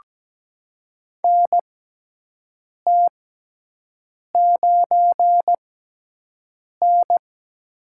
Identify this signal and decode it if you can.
Morse code, heard as a keyed tone, about 17 wpm, spaced as for 6 wpm, 700 Hz. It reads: NT9N
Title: USA Call Signs - 4 Characters - 6x Character Spacing - 17wpm